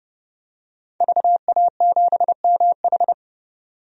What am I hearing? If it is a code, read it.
Morse code, heard as a keyed tone, about 30 wpm, 700 Hz. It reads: VA7MH
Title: DXCC-100 Call Signs - 5 Characters - 30wpm